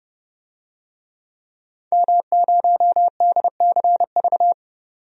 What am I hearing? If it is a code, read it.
Morse code, heard as a keyed tone, about 31 wpm, 690 Hz. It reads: M0DCV
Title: DXCC-100 Call Signs - 5 Characters - 30wpm